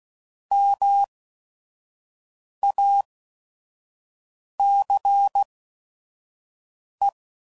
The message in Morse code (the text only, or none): MACE